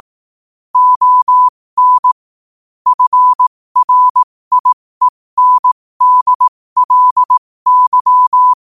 Morse code, heard as a keyed tone, about 18 wpm, spaced as for 15 wpm, 1000 Hz. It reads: ON FRIENDLY
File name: qotdfiles / 15wpm